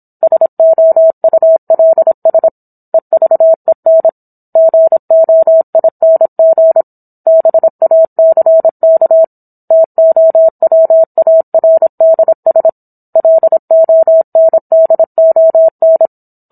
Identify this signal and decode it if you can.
Morse code, heard as a keyed tone, about 27 wpm, 650 Hz. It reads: SOULS EVEN GOING BACK TOWARDS LONDON